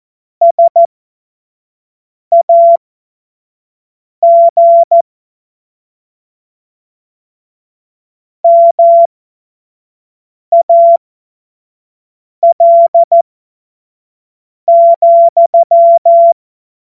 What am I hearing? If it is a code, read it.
Morse code, heard as a keyed tone, about 14 wpm, spaced as for 5 wpm, 680 Hz. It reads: SAG MAL,
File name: Morsekurs / Witze / Witz 09. Februar 2021, 25 F70